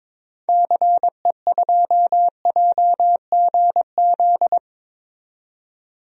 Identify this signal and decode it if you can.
Morse code, heard as a keyed tone, about 21 wpm, 700 Hz. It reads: CE2JGZ
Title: DXCC-100 Call Signs - 6 Characters - 22wpm